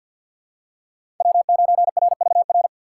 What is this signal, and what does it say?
Morse code, heard as a keyed tone, about 51 wpm, 700 Hz. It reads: W9RVR
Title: USA Call Signs - 5 Characters - 3x Character Spacing - 50wpm